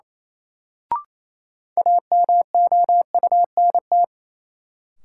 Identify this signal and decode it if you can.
Morse code, heard as a keyed tone, about 28 wpm, 700 Hz. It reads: AMOUNT
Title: Sets of 3 Words - New Word Drill - 28wpm